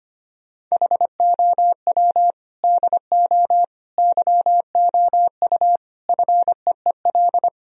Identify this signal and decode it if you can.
Morse code, heard as a keyed tone, about 25 wpm, 700 Hz. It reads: HOW DO YOU FEEL
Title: Sentences from Top 1000 Words - Review Part 2 - 25wpm